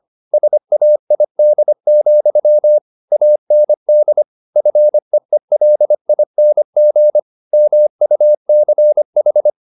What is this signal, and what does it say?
Morse code, heard as a keyed tone, about 25 wpm, 600 Hz. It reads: SAID, AND FEELING MUCH
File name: the old man and the sea part 1 25wpm